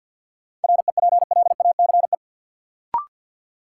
Morse code, heard as a keyed tone, about 52 wpm, 700 Hz. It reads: REPLACE
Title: Sentences from Top 1400 Words - New Word Drill - 50wpm